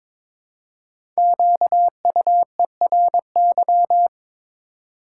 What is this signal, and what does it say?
Morse code, heard as a keyed tone, about 22 wpm, 700 Hz. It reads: QUERY
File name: 5 Letter Words - First 3 Letters Given - 22wpm